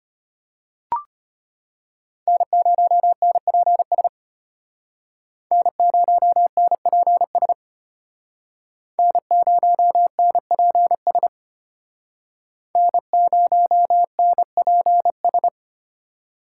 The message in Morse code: N0NPS N0NPS N0NPS N0NPS

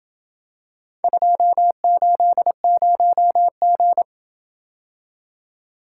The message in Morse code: 280G

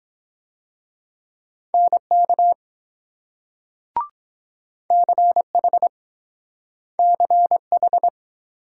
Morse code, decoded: NK CH CH

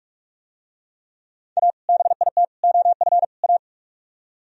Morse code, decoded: A BIT OF A

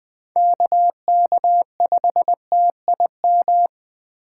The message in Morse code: KK5TIM

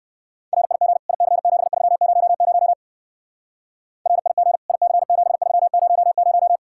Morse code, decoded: RIG IC7300 RIG IC7300

